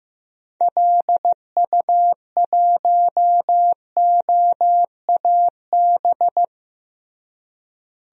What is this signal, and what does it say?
Morse code, heard as a keyed tone, about 15 wpm, 700 Hz. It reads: LU1OAB
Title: DXCC-100 Call Signs - 6 Characters - Repeated 3x - 15wpm